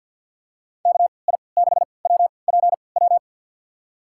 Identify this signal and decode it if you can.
Morse code, heard as a keyed tone, about 51 wpm, spaced as for 28 wpm, 690 Hz. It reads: KI6WPW
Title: USA Call Signs - 6 Characters - 3x Character Spacing - 50wpm